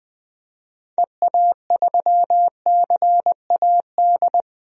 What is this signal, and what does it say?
Morse code, heard as a keyed tone, about 20 wpm, 700 Hz. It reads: EA3CAD
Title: DXCC-100 Call Signs - 6 Characters - Repeated 3x - 20wpm